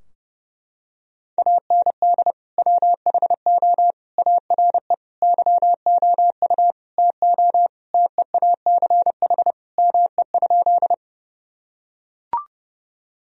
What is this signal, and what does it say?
Morse code, heard as a keyed tone, about 30 wpm, 700 Hz. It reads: AND WHO ARE YOU TO TEACH ME?
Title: Sentences from Top 1000 Words - Review Part 2 - 30wpm